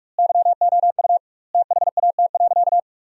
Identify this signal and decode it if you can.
Morse code, heard as a keyed tone, about 45 wpm, 700 Hz. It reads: YOU THAT.
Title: 2x Word-Spacing - Sentences from Top 300 Words - 45wpm